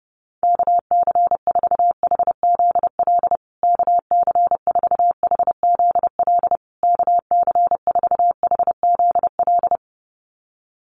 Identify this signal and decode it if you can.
Morse code, heard as a keyed tone, about 30 wpm, 700 Hz. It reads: KC4HZL KC4HZL KC4HZL
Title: USA Call Signs Repeated 3x 30wpm